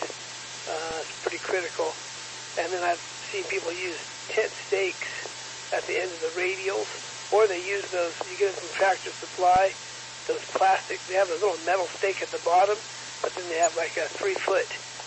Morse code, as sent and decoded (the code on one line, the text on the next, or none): none